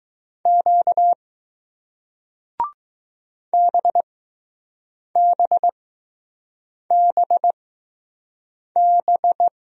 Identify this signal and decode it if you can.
Morse code, heard as a keyed tone, about 20 wpm, spaced as for 6 wpm, 700 Hz. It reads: Q BBBB